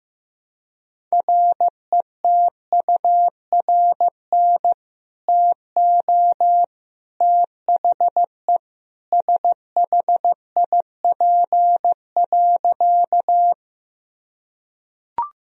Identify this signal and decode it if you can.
Morse code, heard as a keyed tone, about 15 wpm, 700 Hz. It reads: RETURN TO THE SHIP.
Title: Sentences from Top 1000 Words - Review Part 2 - 15wpm